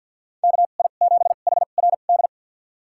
Morse code, heard as a keyed tone, about 50 wpm, spaced as for 36 wpm, 700 Hz. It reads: KI7HRD
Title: USA Call Signs - 2x Character Spacing - 50wpm